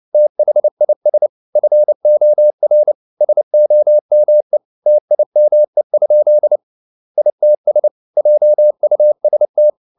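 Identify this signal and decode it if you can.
Morse code, heard as a keyed tone, about 29 wpm, 600 Hz. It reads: THIS FOR SOME TIME? ITS JUST
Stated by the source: Morse code at 29 wpm, sending Ken Follett's novel Never in English, from CW med Gnister.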